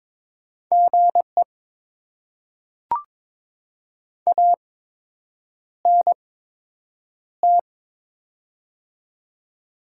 Morse code, heard as a keyed tone, about 22 wpm, spaced as for 8 wpm, 700 Hz. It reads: GE ANT
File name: Top 20 QSO Elements - 8x Character Spacing - 22wpm